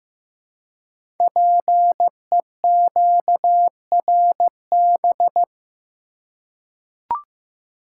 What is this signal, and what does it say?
Morse code, heard as a keyed tone, about 15 wpm, 700 Hz. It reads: PEQRB
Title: Code Groups - 5 Letters - 15wpm